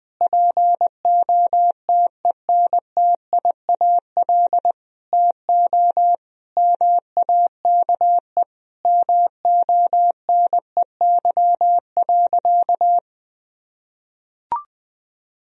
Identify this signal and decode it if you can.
Morse code, heard as a keyed tone, about 20 wpm, 700 Hz. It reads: POTENTIAL TO MAKE MONEY.
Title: Sentences from Top 1000 Words - Review Part 4 - 20wpm